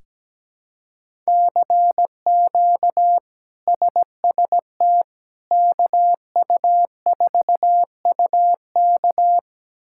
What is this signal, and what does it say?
Morse code, heard as a keyed tone, about 17 wpm, 700 Hz. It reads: CQ SST KU4UK
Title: K1USN SST QSO Practice - 17wpm